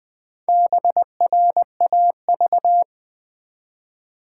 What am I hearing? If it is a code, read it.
Morse code, heard as a keyed tone, about 20 wpm, 700 Hz. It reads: BRAV
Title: Build A Word - 1 Letter At A Time - 5-Letter Words - 20wpm